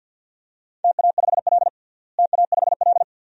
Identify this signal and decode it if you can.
Morse code, heard as a keyed tone, about 48 wpm, 700 Hz. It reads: TA5L TA5L
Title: DXCC-100 Call Signs - 4 Characters - Repeated 2x - 50wpm